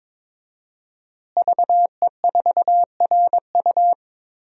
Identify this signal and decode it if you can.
Morse code, heard as a keyed tone, about 22 wpm, 700 Hz. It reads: VE4RU
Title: DXCC-100 Call Signs - 5 Characters - Repeated 2x - 22wpm